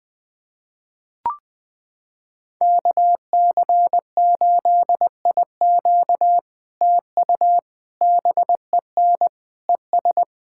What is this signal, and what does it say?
Morse code, heard as a keyed tone, about 20 wpm, 700 Hz. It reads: KC8IQ TU BEN ES
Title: POTA QSO Practice - 20wpm